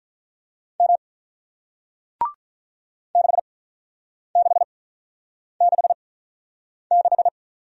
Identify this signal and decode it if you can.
Morse code, heard as a keyed tone, about 47 wpm, spaced as for 9 wpm, 700 Hz. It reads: M 6666